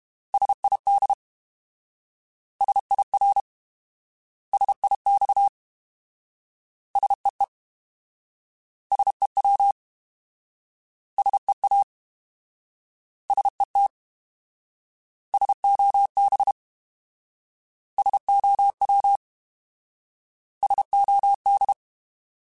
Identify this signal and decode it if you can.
Morse code, heard as a keyed tone, about 32 wpm, 800 Hz. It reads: SID SIR SIX SEE SEW SEA SET SOB SOW SOD
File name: Alphabetical / Alphabetical mp3s / Ss @30